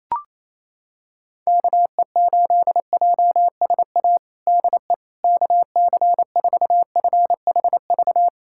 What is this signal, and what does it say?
Morse code, heard as a keyed tone, about 28 wpm, 700 Hz. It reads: KE8JSA DE KC4FHV
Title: USA Callsign DE Callsign - 6 Characters - 28wpm